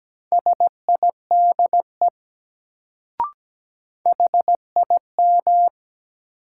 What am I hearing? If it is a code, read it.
Morse code, heard as a keyed tone, about 17 wpm, 700 Hz. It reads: SIDE HIM
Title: Sets of 2 Words - New Word Drill - 17wpm